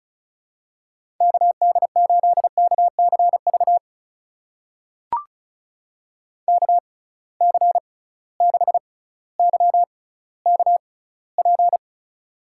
Morse code, heard as a keyed tone, about 34 wpm, spaced as for 18 wpm, 700 Hz. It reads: KD8KCV KC6YKP
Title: USA Call Signs - 6 Characters - 6x Character Spacing - 35wpm